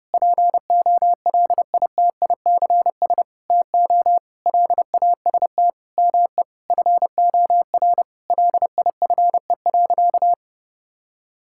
Morse code, heard as a keyed tone, about 30 wpm, 700 Hz. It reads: POLITICS TO LAST ME FOR LIFE.
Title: Sentences from Top 1000 Words - Review Part 6 - 30wpm